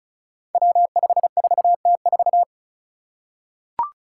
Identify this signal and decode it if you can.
Morse code, heard as a keyed tone, about 36 wpm, 700 Hz. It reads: W54T4
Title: Five Challenging Letter-Number Combinations 35wpm